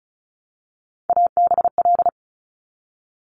Morse code, heard as a keyed tone, about 36 wpm, 700 Hz. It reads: ABL
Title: Top Three Letter Combinations in Words 35wpm